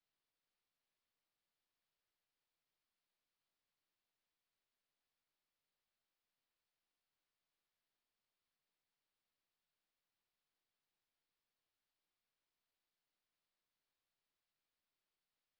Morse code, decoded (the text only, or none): none